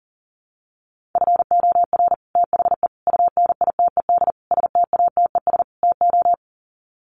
Morse code, heard as a keyed tone, about 40 wpm, 700 Hz. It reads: FOR THE UNITED STATES TO
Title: Sets of 5 Words Encore 40wpm